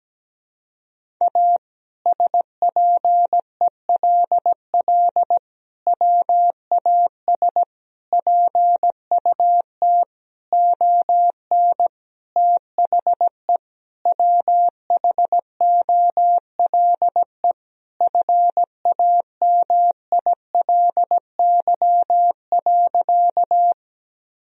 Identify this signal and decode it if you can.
Morse code, heard as a keyed tone, about 17 wpm, 700 Hz. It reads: A SPELL WAS PUT ON THE WHOLE FAMILY.
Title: Sentences from Top 1000 Words - Review Part 1 - 17wpm